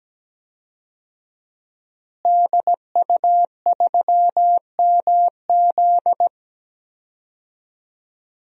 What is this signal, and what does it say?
Morse code, heard as a keyed tone, about 17 wpm, 700 Hz. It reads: DU3MZ
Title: DXCC-100 Call Signs - 5 Characters - Repeated 2x - 17wpm